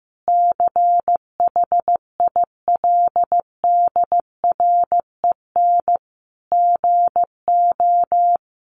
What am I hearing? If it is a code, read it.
Morse code, heard as a keyed tone, about 15 wpm, 700 Hz. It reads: CHILDREN GO
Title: Sentences from Top 1200 Words 15wpm